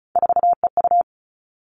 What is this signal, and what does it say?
Morse code, heard as a keyed tone, about 36 wpm, 700 Hz. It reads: 4EU
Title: Three Challenging Letter-Number Combinations 35wpm